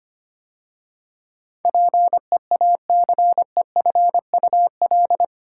Morse code, heard as a keyed tone, about 25 wpm, 700 Hz. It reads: PEACEFUL
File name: 8 Letter Words - First 3 Letters Given - 25wpm